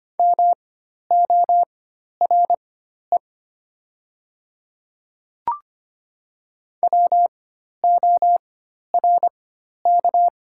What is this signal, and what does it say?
Morse code, heard as a keyed tone, about 25 wpm, spaced as for 12 wpm, 700 Hz. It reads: MORE WORK
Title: 4x Character Spacing - Top 200 Words - 25wpm